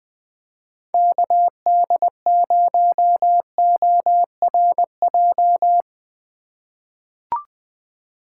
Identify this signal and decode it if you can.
Morse code, heard as a keyed tone, about 20 wpm, 700 Hz. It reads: KD0ORJ
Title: USA Call Signs - 6 Characters - 4x Character Spacing - 20wpm